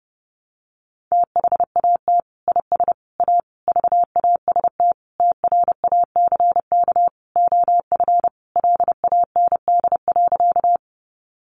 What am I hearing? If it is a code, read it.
Morse code, heard as a keyed tone, about 30 wpm, 700 Hz. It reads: THAT IS A VAST TRACK OF LAND.
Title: Sentences from Top 1300 Words 30wpm